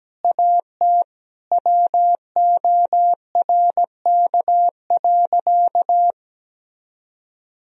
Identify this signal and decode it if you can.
Morse code, heard as a keyed tone, about 17 wpm, 700 Hz. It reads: AT WORK.
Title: Sentences from Top 1000 Words - Review Part 4 - 17wpm